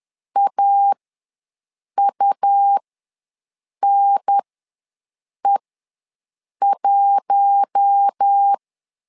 Morse code, heard as a keyed tone, about 10 wpm, spaced as for 6 wpm, 800 Hz. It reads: AUNE1
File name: LEZIONI / 4 Q1 ws 10wpm 2538